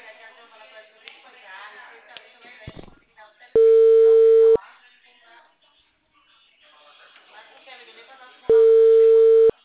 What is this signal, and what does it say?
Ringback tone (China, 450 Hz), with speech.